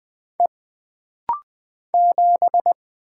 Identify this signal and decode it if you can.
Morse code, heard as a keyed tone, about 20 wpm, 700 Hz. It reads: E 7
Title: Single Letter-Number - Rapid-Fire 20wpm